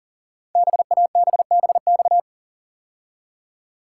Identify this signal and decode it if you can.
Morse code, heard as a keyed tone, about 44 wpm, 700 Hz. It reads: BABBX
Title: Code Groups - 5 Letters - 40wpm